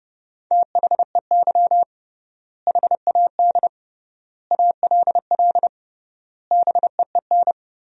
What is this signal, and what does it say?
Morse code, heard as a keyed tone, about 32 wpm, 700 Hz. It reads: THEY HAD ALL BEEN
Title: 3x Word-Spacing - Sentences from Top 200 Words - 30wpm